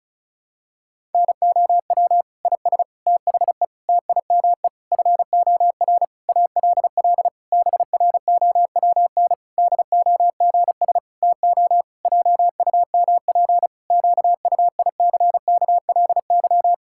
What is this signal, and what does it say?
Morse code, heard as a keyed tone, about 34 wpm, 700 Hz. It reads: NOW IS THE TIME FOR ALL BROWN DOGS TO JUMP QUICKLY
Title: Pangram Challenge v2 - 35wpm